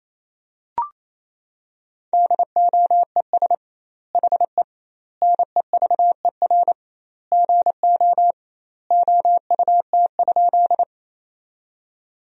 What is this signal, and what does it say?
Morse code, heard as a keyed tone, about 28 wpm, 700 Hz. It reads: DOES HE NEVER GO OUT?
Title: 2x Word-Spacing - Sentences from Top 300 Words - 28wpm